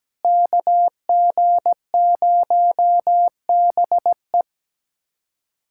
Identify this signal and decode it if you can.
Morse code, heard as a keyed tone, about 17 wpm, 700 Hz. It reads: KG0BE
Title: USA Callsign DE Callsign 17wpm